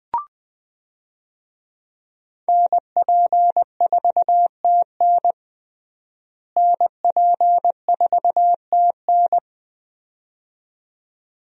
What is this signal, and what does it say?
Morse code, heard as a keyed tone, about 20 wpm, 700 Hz. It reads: NP4TN NP4TN
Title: USA Call Signs - 5 Characters - Repeated 2x - 20wpm